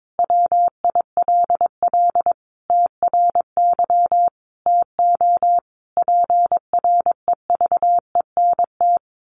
Morse code, heard as a keyed tone, about 22 wpm, 700 Hz. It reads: WILL TRY TO PREVENT
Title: Sentences from Top 1100 Words 22wpm